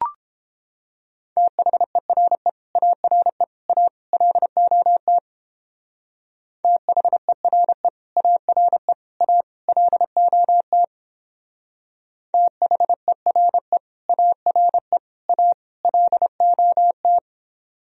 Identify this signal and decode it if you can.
Morse code, heard as a keyed tone, about 31 wpm, 700 Hz. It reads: THERE ARE A LOT THERE ARE A LOT THERE ARE A LOT